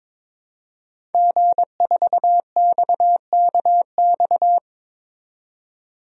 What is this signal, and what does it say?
Morse code, heard as a keyed tone, about 22 wpm, 700 Hz. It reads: G4XKX
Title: DXCC-100 Call Signs - 5 Characters - 22wpm